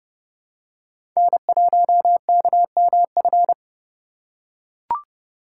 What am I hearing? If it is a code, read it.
Morse code, heard as a keyed tone, about 30 wpm, 700 Hz. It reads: N1KMF
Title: USA Call Signs - 2x Character Spacing - 30wpm